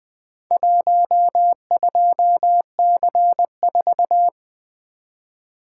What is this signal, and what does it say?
Morse code, heard as a keyed tone, about 24 wpm, 700 Hz. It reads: 12C4
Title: Speed-Racing - Four Letter-Number Combinations - 20wpm